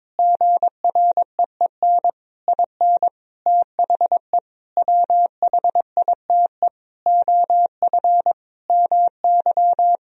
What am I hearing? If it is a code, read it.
Morse code, heard as a keyed tone, about 22 wpm, 700 Hz. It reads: GREEN IN THE WHITE OF MY
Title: Sentences from Top 500 Words and Abbreviations 22wpm